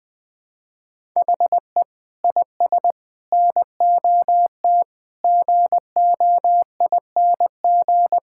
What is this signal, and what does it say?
Morse code, heard as a keyed tone, about 19 wpm, 700 Hz. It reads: HE IS NOT GOING
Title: Sentences with 5-Word Phrases - 20wpm